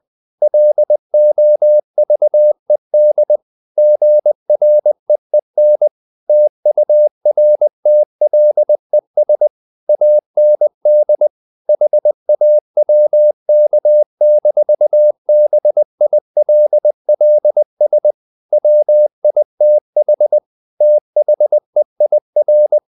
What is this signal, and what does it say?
Morse code, heard as a keyed tone, about 20 wpm, 600 Hz. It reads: LOVED GREEN TURTLES AND HAWK-BILLS WITH THEIR
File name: the old man and the sea / the old man and the sea part 1 20wpm